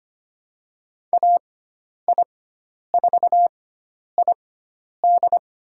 Morse code, heard as a keyed tone, about 25 wpm, spaced as for 10 wpm, 700 Hz. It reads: AI4ID
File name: Lesson 16g - Letter c - Call Signs 25wpm